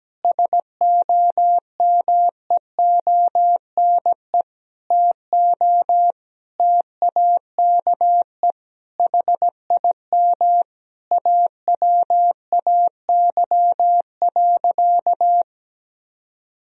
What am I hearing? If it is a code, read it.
Morse code, heard as a keyed tone, about 17 wpm, 700 Hz. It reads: SOMEONE TO TAKE HIM AWAY.
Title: Sentences from Top 1000 Words - Review Part 4 - 17wpm